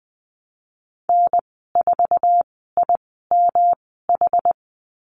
Morse code, heard as a keyed tone, about 20 wpm, spaced as for 14 wpm, 700 Hz. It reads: N4IMH
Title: 128 Lesson 21g - Number 6 - Call Signs 20wpm